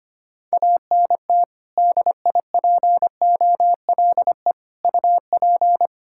Morse code, heard as a keyed tone, about 25 wpm, 700 Hz. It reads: ANT DIPOLE UP
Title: K1USN SST QSO Practice 25wpm